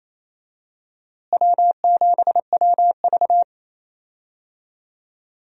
Morse code, heard as a keyed tone, about 28 wpm, 700 Hz. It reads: W7WV
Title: USA Call Signs - 4 Characters - Repeated 3x - 28wpm